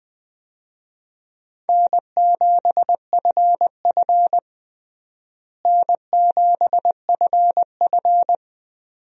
Morse code, heard as a keyed tone, about 19 wpm, 700 Hz. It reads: N7FF N7FF
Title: USA Call Signs - 4 Characters - Repeated 2x - 20wpm